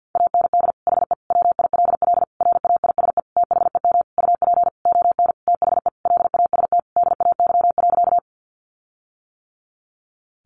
Morse code, heard as a keyed tone, about 50 wpm, 700 Hz. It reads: AND HE WILL RAISE THEM UP ON THE LAST DAY.
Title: Sentences from Top 1400 Words 50wpm